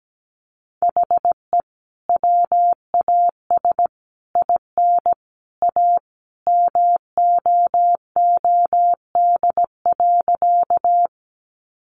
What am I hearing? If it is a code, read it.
Morse code, heard as a keyed tone, about 17 wpm, 700 Hz. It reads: HE WAS IN A MOOD.
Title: Sentences from Top 1100 Words 17wpm